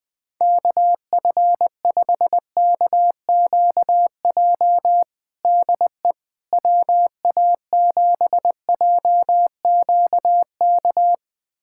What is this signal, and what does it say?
Morse code, heard as a keyed tone, about 20 wpm, 700 Hz. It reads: KF5KQJ DE WA7JQK
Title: USA Callsign DE Callsign 20wpm